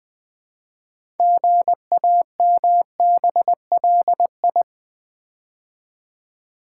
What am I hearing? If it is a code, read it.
Morse code, heard as a keyed tone, about 20 wpm, 700 Hz. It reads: GAMBLI
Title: Build A Word - 2 Letters At A Time - 8-Letter Words - 20wpm